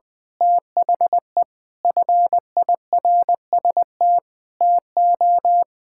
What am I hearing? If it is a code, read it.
Morse code, heard as a keyed tone, about 20 wpm, 700 Hz. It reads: THE FIRST TO
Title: Sentences with 5-Word Phrases - 20wpm